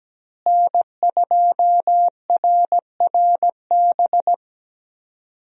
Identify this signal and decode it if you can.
Morse code, heard as a keyed tone, about 17 wpm, 700 Hz. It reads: N2RRB
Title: USA Call Signs - 5 Characters - Repeated 3x - 17wpm